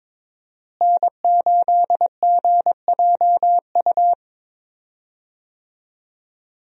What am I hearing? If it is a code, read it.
Morse code, heard as a keyed tone, about 22 wpm, 700 Hz. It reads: N8GJU
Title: USA Call Signs - 5 Characters - Repeated 2x - 22wpm